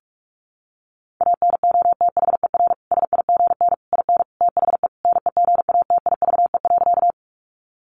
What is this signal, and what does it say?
Morse code, heard as a keyed tone, about 44 wpm, 700 Hz. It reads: ANOTHER SIGN IN THE NEGATIVE.
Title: Sentences from Top 1000 Words 45wpm